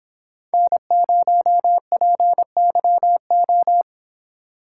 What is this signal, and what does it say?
Morse code, heard as a keyed tone, about 26 wpm, 700 Hz. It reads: N0PYO